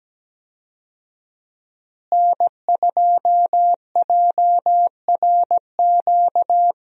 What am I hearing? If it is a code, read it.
Morse code, heard as a keyed tone, about 17 wpm, 700 Hz. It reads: N2JRQ
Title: USA Call Signs - 5 Characters - Repeated 3x - 17wpm